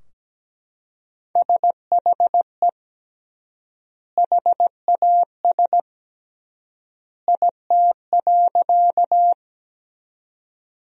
Morse code, heard as a keyed tone, about 17 wpm, 700 Hz. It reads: SHE HAS IT.
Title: 3x Word-Spacing - Sentences from Top 100 Words - 17wpm